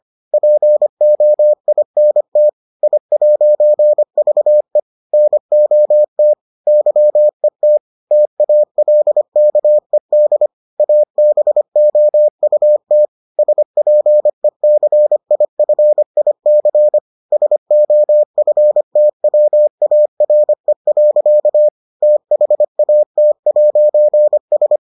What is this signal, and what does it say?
Morse code, heard as a keyed tone, about 25 wpm, 600 Hz. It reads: POINT I'VE NOT YET TALKED ABOUT SPECIFIC SOFTWARE. THAT'S